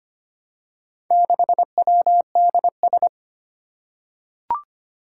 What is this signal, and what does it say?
Morse code, heard as a keyed tone, about 25 wpm, 700 Hz. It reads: 6WDS
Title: Four Challenging Letter-Number Combinations 25wpm